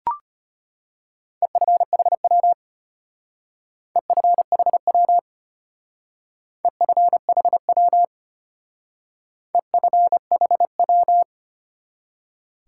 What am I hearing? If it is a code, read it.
Morse code, heard as a keyed tone, about 31 wpm, 700 Hz. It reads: EFHW EFHW EFHW EFHW
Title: Speed-Racing Top 180 QSO Elements 25wpm